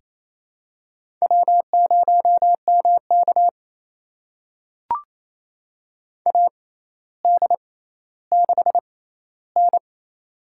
Morse code, heard as a keyed tone, about 28 wpm, spaced as for 14 wpm, 700 Hz. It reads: W0MK AD6N
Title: USA Call Signs - 4 Characters - 6x Character Spacing - 28wpm